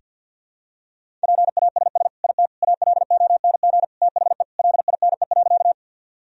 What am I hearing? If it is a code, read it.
Morse code, heard as a keyed tone, about 52 wpm, 700 Hz. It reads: PASS IT ALONG THE LINE.
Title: Sentences from Top 1000 Words - Review Part 1 - 50wpm